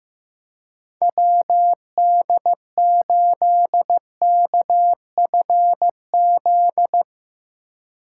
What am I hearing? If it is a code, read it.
Morse code, heard as a keyed tone, about 15 wpm, 700 Hz. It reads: WD8KFZ
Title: USA Call Signs - 6 Characters - 2x Character Spacing - 15wpm